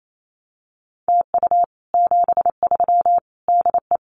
Morse code, heard as a keyed tone, about 28 wpm, 700 Hz. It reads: TU 73 DE
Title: Common QSO Phrases Encore 28wpm